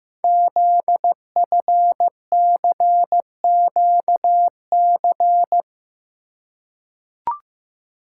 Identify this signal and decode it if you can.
Morse code, heard as a keyed tone, about 15 wpm, 700 Hz. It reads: ZFCQC
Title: Code Groups - 5 Letters - 15wpm